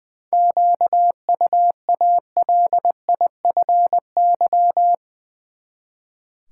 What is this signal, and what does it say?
Morse code, heard as a keyed tone, about 21 wpm, 700 Hz. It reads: QUALIFY
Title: Seven Letter Words - 20wpm